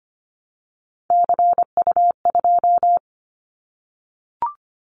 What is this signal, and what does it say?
Morse code, heard as a keyed tone, about 25 wpm, 700 Hz. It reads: CU2
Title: Callsign Prefixes for Top 50 DXCC - 25wpm